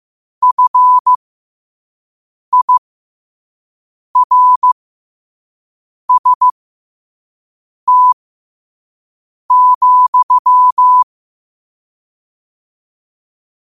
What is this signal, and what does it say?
Morse code, heard as a keyed tone, about 15 wpm, spaced as for 5 wpm, 1000 Hz. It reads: FIRST,
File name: qotdfiles / 5wpm